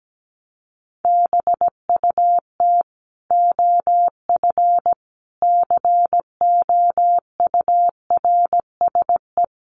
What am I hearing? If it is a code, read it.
Morse code, heard as a keyed tone, about 17 wpm, 700 Hz. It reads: BUT OF COURSE
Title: Sentences from Top 1500 Words 17wpm